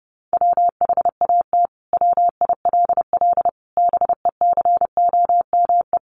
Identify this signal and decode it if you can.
Morse code, heard as a keyed tone, about 30 wpm, 700 Hz. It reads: WHAT WILL BECOME
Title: Sentences from Top 600 Words 30wpm